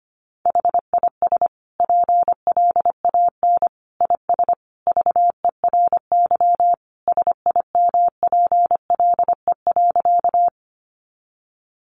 Morse code, heard as a keyed tone, about 25 wpm, 700 Hz. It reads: HIS PLAN IS VERY SIMPLE.
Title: Sentences from Top 500 Words 25wpm